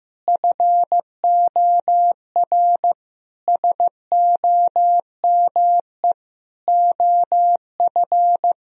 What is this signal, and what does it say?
Morse code, heard as a keyed tone, about 15 wpm, 700 Hz. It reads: FOR SOME OF
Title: Sentences from Top 1000 Words - Review Part 3 - 15wpm